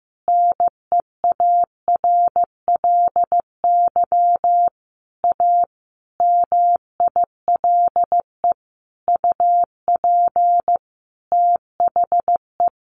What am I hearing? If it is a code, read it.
Morse code, heard as a keyed tone, about 15 wpm, 700 Hz. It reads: NEARLY A MILE UP THE
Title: Sentences from Top 1300 Words 15wpm